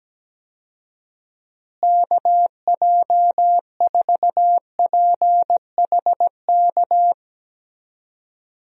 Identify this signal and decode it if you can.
Morse code, heard as a keyed tone, about 17 wpm, 700 Hz. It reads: KJ4PHK
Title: USA Call Signs - 6 Characters - Repeated 3x - 17wpm